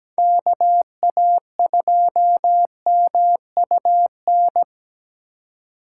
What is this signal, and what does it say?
Morse code, heard as a keyed tone, about 20 wpm, 700 Hz. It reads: KA2MUN
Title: Speed Racing - USA 2x3 Call Signs 17wpm